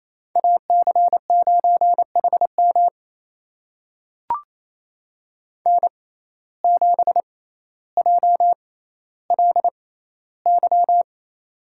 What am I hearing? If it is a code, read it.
Morse code, heard as a keyed tone, about 27 wpm, spaced as for 14 wpm, 700 Hz. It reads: AC9HM N7JLY
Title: USA Call Signs - 5 Characters - 6x Character Spacing - 28wpm